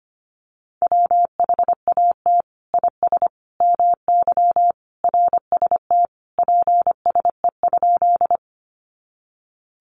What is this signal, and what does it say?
Morse code, heard as a keyed tone, about 25 wpm, 700 Hz. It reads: WHAT IS MY RST PSE?
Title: Common QSO Phrases 25wpm